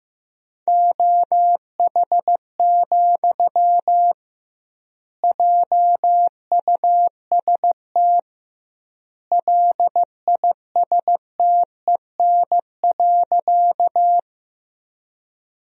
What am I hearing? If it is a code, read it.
Morse code, heard as a keyed tone, about 16 wpm, 700 Hz. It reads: OH, JUST LISTEN.
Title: 2x Word-Spacing - Sentences from Top 500 Words - 15wpm